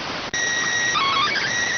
SSTV (slow-scan television), PD 290.